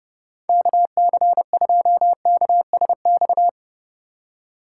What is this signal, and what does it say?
Morse code, heard as a keyed tone, about 30 wpm, 700 Hz. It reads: KC2KSX
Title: USA Call Signs - 6 Characters - 2x Character Spacing - 30wpm